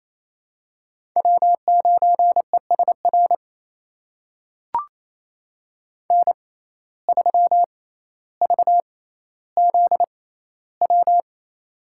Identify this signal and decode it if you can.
Morse code, heard as a keyed tone, about 28 wpm, spaced as for 14 wpm, 700 Hz. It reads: W9ESR N3VZW
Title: USA Call Signs - 5 Characters - 6x Character Spacing - 28wpm